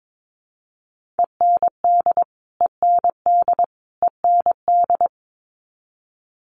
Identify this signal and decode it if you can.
Morse code, heard as a keyed tone, about 22 wpm, 700 Hz. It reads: END END END
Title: Top 500 Words Repeated 22wpm